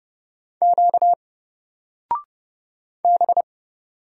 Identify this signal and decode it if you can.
Morse code, heard as a keyed tone, about 30 wpm, 700 Hz. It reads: Q B
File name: Single-Letter-Number-30wpm